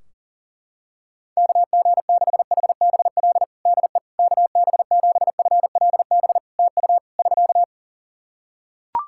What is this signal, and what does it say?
Morse code, heard as a keyed tone, about 38 wpm, 700 Hz. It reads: KG6HBL DE KB7FLB TU <SK>
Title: Practice QSO - Exchange 4 - Ending - 40wpm